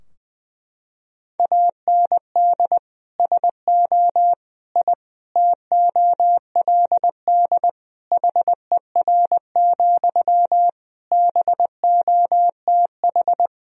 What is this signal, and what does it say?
Morse code, heard as a keyed tone, about 20 wpm, 700 Hz. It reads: AND SO I TOLD HER, BOTH
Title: Sentences from Top 1000 Words - Review Part 4 - 20wpm